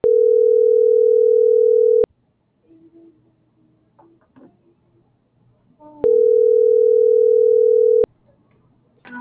Ringback tone (North America), 440 + 480 Hz.